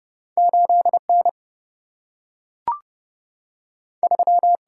Morse code, heard as a keyed tone, about 32 wpm, 700 Hz. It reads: 8 N 3
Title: Single-Letter Single-Number 30wpm